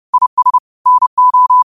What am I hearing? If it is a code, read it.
Morse code, heard as a keyed tone, about 30 wpm, 1000 Hz. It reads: IS NO